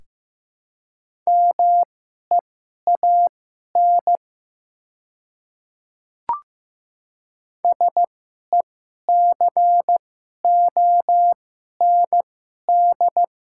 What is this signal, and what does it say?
Morse code, heard as a keyed tone, about 15 wpm, spaced as for 11 wpm, 700 Hz. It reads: MEAN SECOND
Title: 2x Character Spacing - Top 300 Words - 15wpm